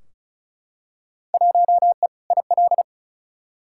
Morse code, heard as a keyed tone, about 36 wpm, 700 Hz. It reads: 1E IL
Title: Field Day Exchanges - 35wpm